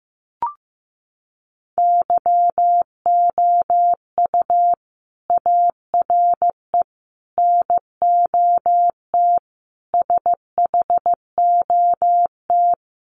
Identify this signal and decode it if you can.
Morse code, heard as a keyed tone, about 15 wpm, 700 Hz. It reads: YOU ARE NOT SHOT
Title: Sentences from Top 1100 Words 15wpm